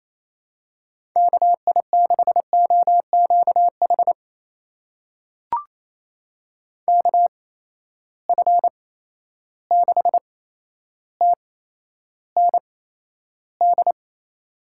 Morse code, heard as a keyed tone, about 27 wpm, spaced as for 12 wpm, 700 Hz. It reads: KI6OQH KF6TND
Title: USA Call Signs - 6 Characters - 8x Character Spacing - 28wpm